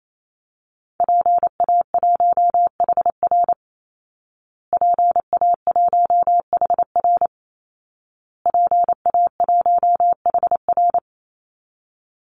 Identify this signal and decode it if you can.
Morse code, heard as a keyed tone, about 28 wpm, 700 Hz. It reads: PA1HR PA1HR PA1HR
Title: Call Signs Repeated 2x - 28wpm